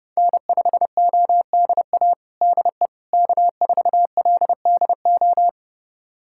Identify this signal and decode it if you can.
Morse code, heard as a keyed tone, about 30 wpm, 700 Hz. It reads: N5ODA DE K4LDO
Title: USA Callsign DE Callsign - 5 Characters - 30wpm